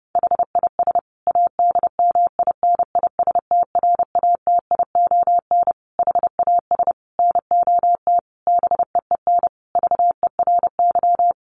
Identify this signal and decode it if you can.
Morse code, heard as a keyed tone, about 30 wpm, 700 Hz. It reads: HIS ADMINISTRATION HAS NOT BEEN VERY
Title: Sentences from Top 1300 Words 30wpm